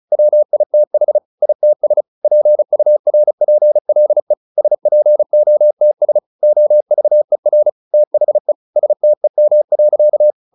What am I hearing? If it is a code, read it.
Morse code, heard as a keyed tone, about 34 wpm, 600 Hz. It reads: WITH ITS PURPLE SPOTS OVER THE STEM.
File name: the old man and the sea 2 35wpm